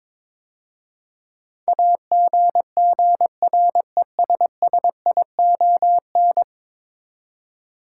Morse code, heard as a keyed tone, about 21 wpm, 700 Hz. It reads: AGGRESSION
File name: practice / 22wpm / 10 Letter Words - First 3 Letters Given - 22wpm